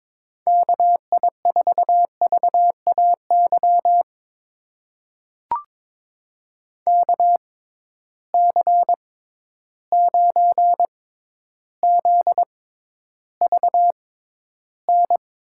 Morse code, heard as a keyed tone, about 22 wpm, spaced as for 11 wpm, 700 Hz. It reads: KI4VAY KC9ZVN
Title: USA Call Signs - 6x Character Spacing - 22wpm